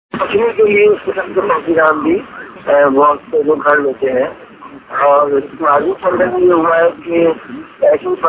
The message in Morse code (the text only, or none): none